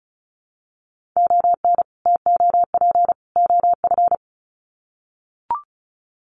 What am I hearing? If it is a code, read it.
Morse code, heard as a keyed tone, about 34 wpm, 700 Hz. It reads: ON TOP OF
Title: Sets of 3 Words 35wpm